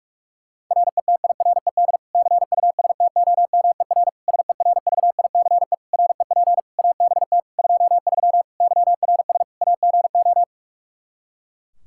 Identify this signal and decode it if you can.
Morse code, heard as a keyed tone, about 41 wpm, 700 Hz. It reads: RETIRED CUSTOMER SERVICE REP ABT 23 YRS AGO
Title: K1USN SST QSO Practice 45wpm